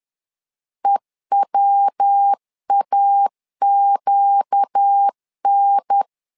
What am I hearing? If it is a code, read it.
Morse code, heard as a keyed tone, about 10 wpm, 800 Hz. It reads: EWAQN
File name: LEZIONI / 4 Q 10wpm 2210